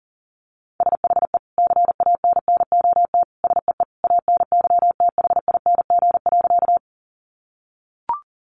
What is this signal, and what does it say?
Morse code, heard as a keyed tone, about 40 wpm, 700 Hz. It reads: SHE CANNOT SEE ANYTHING.
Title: Additional Easy Phrases 40wpm